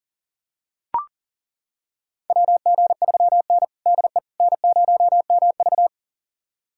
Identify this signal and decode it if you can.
Morse code, heard as a keyed tone, about 40 wpm, 700 Hz. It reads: WG3N DE N0MV
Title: USA Callsign DE Callsign - 4 Characters - 40wpm